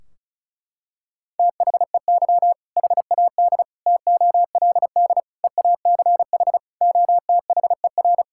Morse code, heard as a keyed tone, about 34 wpm, 700 Hz. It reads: THEY HAD TOLD EACH OTHER
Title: Sentences from Top 1600 Words 35wpm